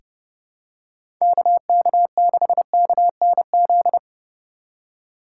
Morse code, heard as a keyed tone, about 32 wpm, 700 Hz. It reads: KK6KNZ
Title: USA Call Signs - 4x Character Spacing - 30wpm